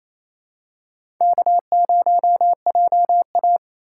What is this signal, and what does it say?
Morse code, heard as a keyed tone, about 28 wpm, 700 Hz. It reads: K0JA
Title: USA Call Signs - 4 Characters - 4x Character Spacing - 28wpm